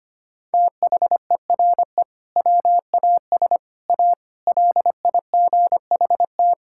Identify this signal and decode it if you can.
Morse code, heard as a keyed tone, about 25 wpm, 700 Hz. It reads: THERE WAS A LIGHT
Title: Sentences from Top 1000 Words - Review Part 6 - 25wpm